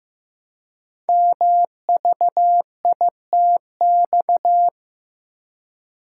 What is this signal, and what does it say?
Morse code, heard as a keyed tone, about 15 wpm, 700 Hz. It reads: MVITX
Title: Code Groups - 5 Letters - 15wpm